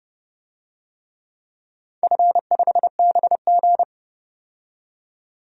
Morse code, heard as a keyed tone, about 30 wpm, 700 Hz. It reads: F5BG
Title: DXCC-100 Call Signs - 4 Characters - 30wpm